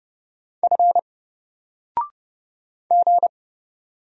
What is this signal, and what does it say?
Morse code, heard as a keyed tone, about 29 wpm, 700 Hz. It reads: F G
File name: Single-Letter-Number-30wpm